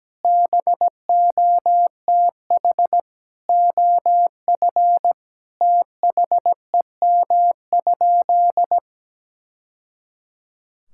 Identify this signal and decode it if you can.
Morse code, heard as a keyed tone, about 17 wpm, 700 Hz. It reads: BOTH OF THEM?
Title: Sentences from Top 1000 Words - Review Part 5 - 17wpm